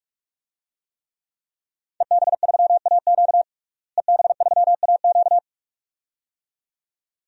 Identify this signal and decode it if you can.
Morse code, heard as a keyed tone, about 47 wpm, 700 Hz. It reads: EB3AQ EB3AQ
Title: DXCC-100 Call Signs - 5 Characters - Repeated 2x - 45wpm